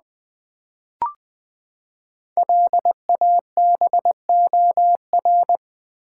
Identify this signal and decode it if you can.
Morse code, heard as a keyed tone, about 20 wpm, 700 Hz. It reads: LABOR
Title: Build A Word - 5 Letters At A Time - 10-Letter Words - 20wpm